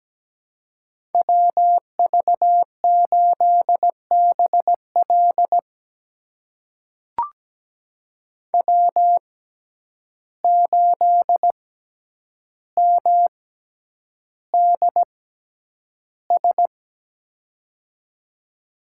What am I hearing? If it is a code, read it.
Morse code, heard as a keyed tone, about 17 wpm, spaced as for 9 wpm, 700 Hz. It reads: WV8BL W8MDS
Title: USA Call Signs - 5 Characters - 6x Character Spacing - 17wpm